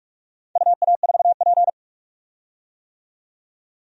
Morse code, heard as a keyed tone, about 46 wpm, 700 Hz. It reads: UA4P
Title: DXCC-100 Call Signs - 4 Characters - 45wpm